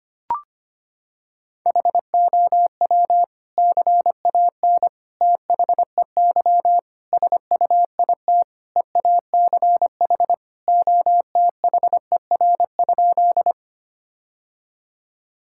Morse code, heard as a keyed tone, about 25 wpm, 700 Hz. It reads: HOW CAN THEY SUIT EACH OTHER?
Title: Sentences from Top 1600 Words 25wpm